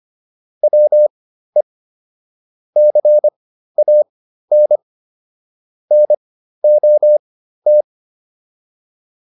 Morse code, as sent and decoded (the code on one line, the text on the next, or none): .-- . / -.-. .- -. / -. --- -
WE CAN NOT